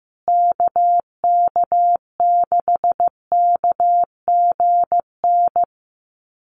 Morse code, heard as a keyed tone, about 15 wpm, 700 Hz. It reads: KK6KGN
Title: USA Call Signs Repeated 3x 15wpm